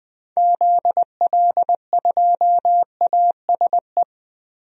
Morse code, heard as a keyed tone, about 20 wpm, 700 Hz. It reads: ZL2ASE
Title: DXCC-100 Call Signs - 6 Characters - 20wpm